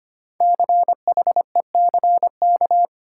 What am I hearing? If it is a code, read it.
Morse code, heard as a keyed tone, about 25 wpm, 700 Hz. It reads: CHECK
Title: Sentences from Top 1000 Words - Review Part 3 - New Word Drill - 25wpm